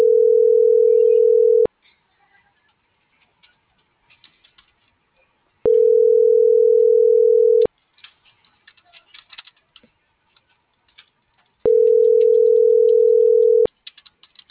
Ringback tone (North America), 440 + 480 Hz.